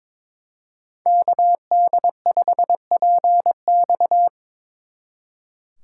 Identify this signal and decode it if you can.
Morse code, heard as a keyed tone, about 27 wpm, 700 Hz. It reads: KD5PX
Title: Speed Racing - USA 2x2 Call Signs 22wpm